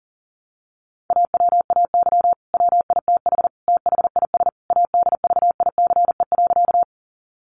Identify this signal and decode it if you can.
Morse code, heard as a keyed tone, about 40 wpm, 700 Hz. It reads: AWAY WITH THIS ADVICE.
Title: Sentences from Top 1100 Words 40wpm